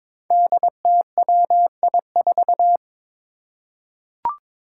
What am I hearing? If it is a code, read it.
Morse code, heard as a keyed tone, about 22 wpm, 700 Hz. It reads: DTWI4
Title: Five Challenging Letter-Number Combinations 22wpm